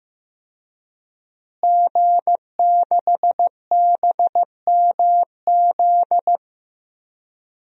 Morse code, heard as a keyed tone, about 15 wpm, 700 Hz. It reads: G6BMZ
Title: DXCC-100 Call Signs - 5 Characters - Repeated 3x - 15wpm